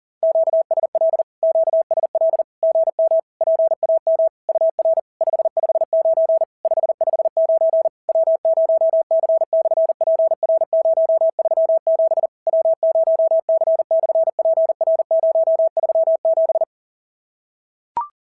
Morse code, heard as a keyed tone, about 40 wpm, 650 Hz. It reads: QSL QSL GM PAM UR 559 559 W0C/PR037 W0C/PR037
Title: SOTA QSO Practice 40wpm